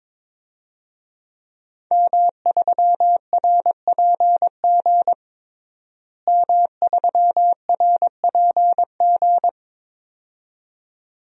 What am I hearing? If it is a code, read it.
Morse code, heard as a keyed tone, about 21 wpm, 700 Hz. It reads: M3RPG M3RPG
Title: DXCC-100 Call Signs - 5 Characters - Repeated 2x - 22wpm